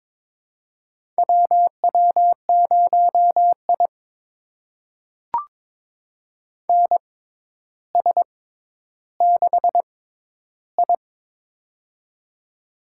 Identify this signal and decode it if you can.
Morse code, heard as a keyed tone, about 22 wpm, spaced as for 11 wpm, 700 Hz. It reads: WW0I NS6I
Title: USA Call Signs - 6x Character Spacing - 22wpm